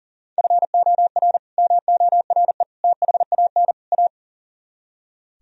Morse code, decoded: FOR MORE THAN A

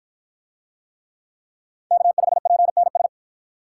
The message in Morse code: K5PNS

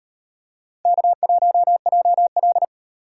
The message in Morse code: K1JL

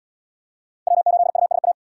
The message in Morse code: W8USA

